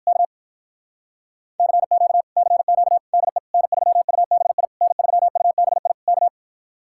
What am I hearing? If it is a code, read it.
Morse code, heard as a keyed tone, about 67 wpm, 700 Hz. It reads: K CQ CQ DE N3UBI N3UBI K